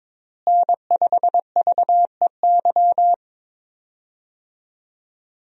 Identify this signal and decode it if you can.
Morse code, heard as a keyed tone, about 22 wpm, 700 Hz. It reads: N5VEY
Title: USA Call Signs - 5 Characters - Repeated 3x - 22wpm